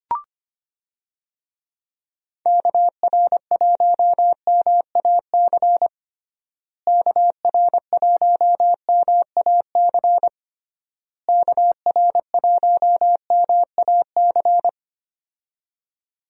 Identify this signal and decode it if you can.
Morse code, heard as a keyed tone, about 25 wpm, 700 Hz. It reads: KR1MAC KR1MAC KR1MAC
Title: USA Call Signs - 6 Characters - Repeated 3x - 25wpm